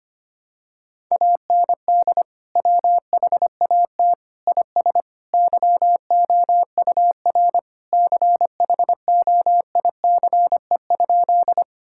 Morse code, heard as a keyed tone, about 25 wpm, 700 Hz. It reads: AND WHAT IS YOUR CHOICE?
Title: Sentences from Top 1000 Words - Review Part 6 - 25wpm